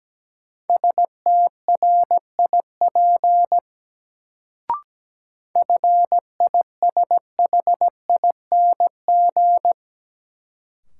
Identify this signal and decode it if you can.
Morse code, heard as a keyed tone, about 17 wpm, 700 Hz. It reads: STRIP FISHING
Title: Sentences from Top 1600 Words - New Word Drill Alt - 17wpm